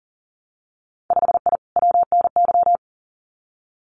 Morse code, heard as a keyed tone, about 40 wpm, 700 Hz. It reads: 5I WNY
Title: Winter Field Day Exchanges Encore 40wpm